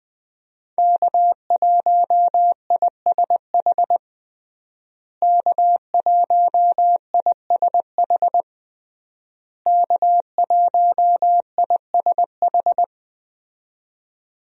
Morse code, heard as a keyed tone, about 20 wpm, 700 Hz. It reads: K1ISH K1ISH K1ISH
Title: USA Call Signs - 5 Characters - Repeated 3x - 20wpm